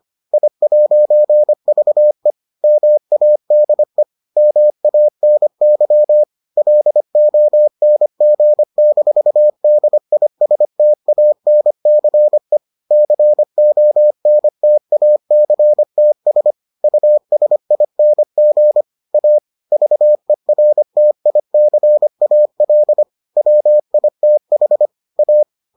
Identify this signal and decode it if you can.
Morse code, heard as a keyed tone, about 25 wpm, 600 Hz. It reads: I'VE MADE MANY LONG-DISTANCE CONTACTS USING A VERTICAL WITH A